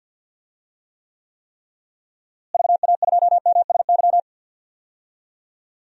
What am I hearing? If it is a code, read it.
Morse code, heard as a keyed tone, about 50 wpm, 700 Hz. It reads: VA2MSY
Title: DXCC-100 Call Signs - 6 Characters - Repeated 3x - 50wpm